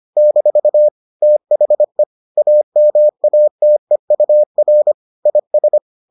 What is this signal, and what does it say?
Morse code, heard as a keyed tone, about 25 wpm, 600 Hz. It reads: - THE AMATEUR IS